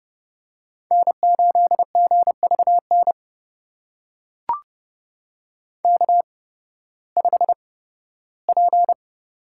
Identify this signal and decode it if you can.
Morse code, heard as a keyed tone, about 30 wpm, spaced as for 16 wpm, 700 Hz. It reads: N8GVN K5P
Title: USA Call Signs - 5 Characters - 8x Character Spacing - 30wpm